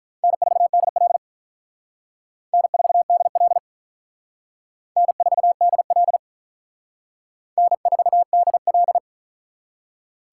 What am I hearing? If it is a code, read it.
Morse code, heard as a keyed tone, about 43 wpm, 700 Hz. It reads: N4DL N4DL N4DL N4DL
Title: Speed Racing - USA 1x2 Call Signs 35wpm